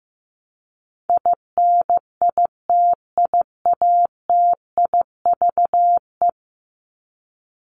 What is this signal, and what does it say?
Morse code, heard as a keyed tone, about 15 wpm, 700 Hz. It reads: INITIATIVE
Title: Top 1751-2000 Words 15wpm